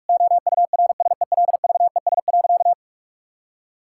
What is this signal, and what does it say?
Morse code, heard as a keyed tone, about 41 wpm, 700 Hz. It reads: OURSELVES.
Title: Sentences from Top 1600 Words 45wpm